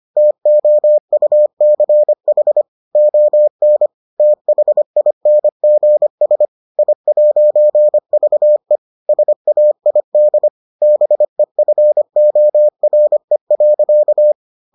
Morse code, heard as a keyed tone, about 25 wpm, 600 Hz. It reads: TOUCH ON THINGS I'VE SAID BEFORE.